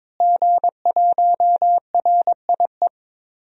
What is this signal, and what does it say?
Morse code, heard as a keyed tone, about 21 wpm, 700 Hz. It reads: G1RIE
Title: DXCC-100 Call Signs - 5 Characters - Repeated 3x - 22wpm